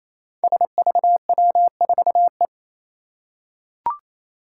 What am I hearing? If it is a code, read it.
Morse code, heard as a keyed tone, about 27 wpm, 700 Hz. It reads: SVW4E
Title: Five Challenging Letter-Number Combinations 28wpm